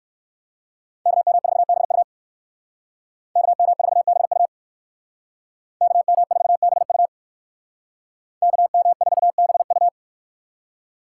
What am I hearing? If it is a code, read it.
Morse code, heard as a keyed tone, about 54 wpm, 700 Hz. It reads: KM4BU KM4BU KM4BU KM4BU
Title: Speed Racing - USA 2x2 Call Signs 45wpm